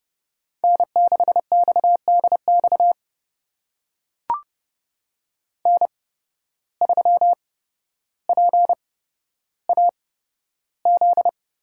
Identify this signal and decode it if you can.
Morse code, heard as a keyed tone, about 30 wpm, spaced as for 13 wpm, 700 Hz. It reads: N6XDX N3PAZ
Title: USA Call Signs - 5 Characters - 8x Character Spacing - 30wpm